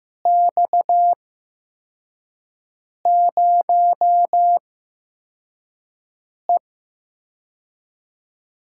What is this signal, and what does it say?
Morse code, heard as a keyed tone, about 15 wpm, spaced as for 4 wpm, 700 Hz. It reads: X0E
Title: USA Call Signs - 4 Characters - 8x Character Spacing - 15wpm